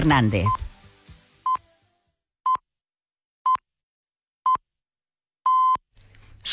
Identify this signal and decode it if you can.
Radio time signal (time pips), with speech. It is heard from 0.5 to 5.8 s.